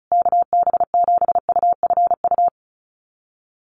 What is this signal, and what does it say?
Morse code, heard as a keyed tone, about 34 wpm, 700 Hz. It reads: KB7UFU
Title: USA Call Signs Repeated 3x 35wpm